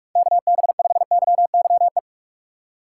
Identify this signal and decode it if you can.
Morse code, heard as a keyed tone, about 58 wpm, 700 Hz. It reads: KB5YYE